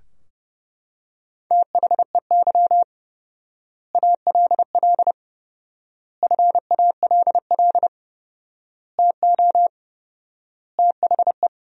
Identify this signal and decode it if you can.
Morse code, heard as a keyed tone, about 29 wpm, 700 Hz. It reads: THEY ALL FALL TO THE
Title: 4x Word-Spacing - Sentences from Top 500 Words - 30wpm